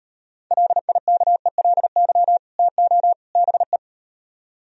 Morse code, heard as a keyed tone, about 38 wpm, 700 Hz. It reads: LIKELY TO BE